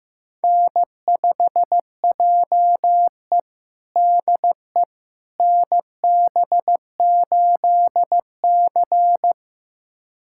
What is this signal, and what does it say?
Morse code, heard as a keyed tone, about 15 wpm, 700 Hz. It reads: N5JE DE NB8C
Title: USA Callsign DE Callsign - 4 Characters - 15wpm